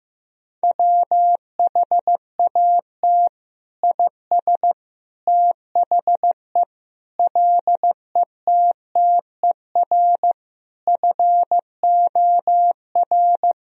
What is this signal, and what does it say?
Morse code, heard as a keyed tone, about 15 wpm, 700 Hz. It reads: WHAT IS THE LETTER FOR
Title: Sentences from Top 1000 Words - Review Part 2 - 15wpm